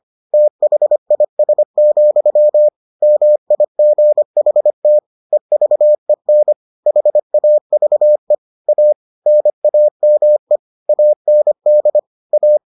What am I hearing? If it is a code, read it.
Morse code, heard as a keyed tone, about 25 wpm, 600 Hz. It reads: THIS, MIGHT EVEN HAVE A NAME AND A